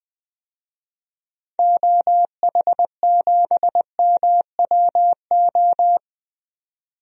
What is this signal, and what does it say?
Morse code, heard as a keyed tone, about 20 wpm, 700 Hz. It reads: OH7MWO
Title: DXCC-100 Call Signs - 6 Characters - 20wpm